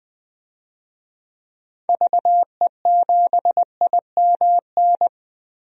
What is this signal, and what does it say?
Morse code, heard as a keyed tone, about 20 wpm, 700 Hz. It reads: VE7IMN
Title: DXCC-100 Call Signs - 6 Characters - Repeated 2x - 20wpm